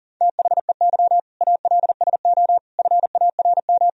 Morse code, heard as a keyed tone, about 40 wpm, 700 Hz. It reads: THEY ALSO FARM
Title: Sentences from Top 1000 Words - Review Part 2 - 40wpm